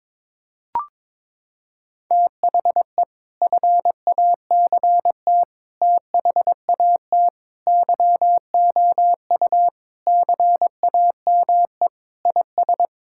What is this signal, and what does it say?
Morse code, heard as a keyed tone, about 21 wpm, 700 Hz. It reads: THE FACT THAT YOU CAME IS
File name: practice / 22wpm / Sentences with 3-Word Phrases - 22wpm